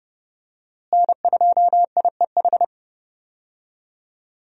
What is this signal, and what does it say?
Morse code, heard as a keyed tone, about 32 wpm, 700 Hz. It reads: N2IEH
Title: USA Call Signs - 5 Characters - Repeated 3x - 30wpm